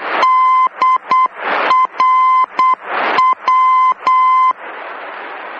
Morse code, heard as a keyed tone, about 8 wpm, 1030 Hz. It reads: DRW
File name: germany / mx / DRW-D